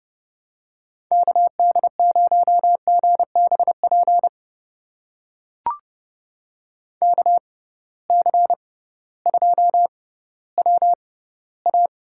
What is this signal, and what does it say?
Morse code, heard as a keyed tone, about 30 wpm, spaced as for 16 wpm, 700 Hz. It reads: KD0GBP KC2WA
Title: USA Call Signs - 6 Characters - 6x Character Spacing - 30wpm